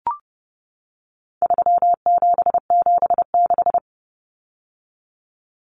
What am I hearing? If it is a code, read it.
Morse code, heard as a keyed tone, about 30 wpm, 700 Hz. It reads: 3776
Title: Numbers - 4-Digit 30wpm